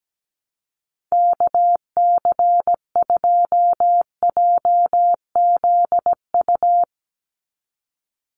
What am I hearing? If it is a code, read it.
Morse code, heard as a keyed tone, about 17 wpm, 700 Hz. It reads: KC2JZU
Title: USA Call Signs 17wpm